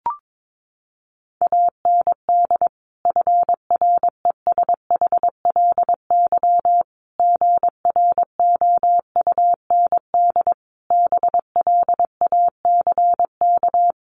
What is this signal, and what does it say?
Morse code, heard as a keyed tone, about 22 wpm, 700 Hz. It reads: AND FRESHLY GROUND BLACK
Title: Sets of 5 Words Encore 22wpm